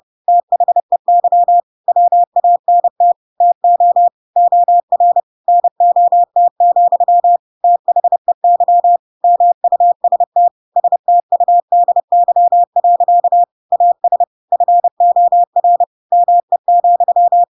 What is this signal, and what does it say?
Morse code, heard as a keyed tone, about 30 wpm, 700 Hz. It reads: THEY WANT TO OR NOT, THEY MUST STUDY. AS FOR ME,
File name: Pinocchio / Pinocchio ch4 30 30 ik1hge